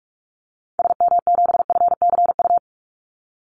Morse code, heard as a keyed tone, about 44 wpm, 700 Hz. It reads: SM7FCU
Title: Call Signs Repeated 2x - 45wpm